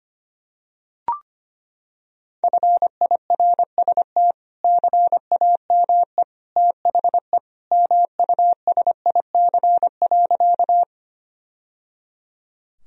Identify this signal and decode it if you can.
Morse code, heard as a keyed tone, about 25 wpm, 700 Hz. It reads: FIRST CAME THE MUSIC.
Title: Sentences from Top 1000 Words - Review Part 1 - 25wpm